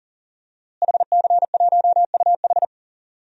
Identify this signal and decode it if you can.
Morse code, heard as a keyed tone, about 40 wpm, 700 Hz. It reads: HC1UH
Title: DXCC-100 Call Signs - 5 Characters - Repeated 3x - 40wpm